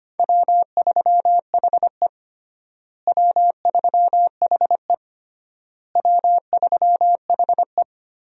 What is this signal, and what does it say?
Morse code, heard as a keyed tone, about 25 wpm, 700 Hz. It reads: W3HE W3HE W3HE
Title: USA Call Signs - 4 Characters - Repeated 3x - 25wpm